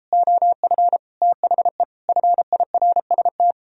Morse code, heard as a keyed tone, about 31 wpm, 700 Hz. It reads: OF THE FIRST